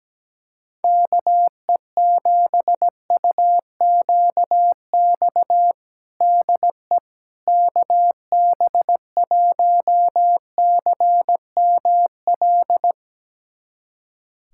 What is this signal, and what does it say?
Morse code, heard as a keyed tone, about 17 wpm, 700 Hz. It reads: KE7UQX DE KB1CML
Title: USA Callsign DE Callsign 17wpm